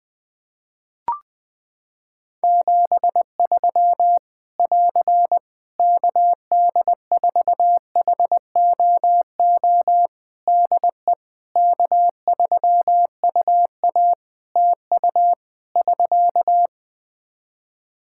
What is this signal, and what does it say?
Morse code, heard as a keyed tone, about 20 wpm, 700 Hz. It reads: 73 <AR> KD4HOO DE K3UA TU <SK>
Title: Practice QSO - Exchange 4 - Ending - 20wpm